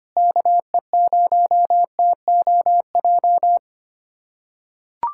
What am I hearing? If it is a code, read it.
Morse code, heard as a keyed tone, about 25 wpm, 700 Hz. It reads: KE0TOJ
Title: USA Callsign DE Callsign 25wpm